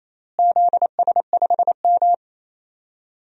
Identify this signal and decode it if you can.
Morse code, heard as a keyed tone, about 28 wpm, 700 Hz. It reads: ZS5M
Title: DXCC-100 Call Signs - 4 Characters - Repeated 2x - 28wpm